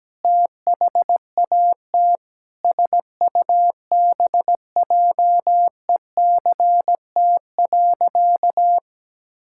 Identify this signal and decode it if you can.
Morse code, heard as a keyed tone, about 17 wpm, 700 Hz. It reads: THAT SUBJECT.
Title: Sentences from Top 1600 Words 17wpm